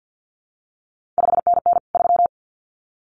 Morse code, heard as a keyed tone, about 50 wpm, 700 Hz. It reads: 5NN 3
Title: CQ WW DX Contest Exchange - 50wpm